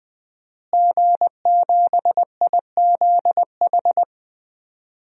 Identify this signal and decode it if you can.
Morse code, heard as a keyed tone, about 20 wpm, 700 Hz. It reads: G7IZH
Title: DXCC-100 Call Signs - 5 Characters - Repeated 2x - 20wpm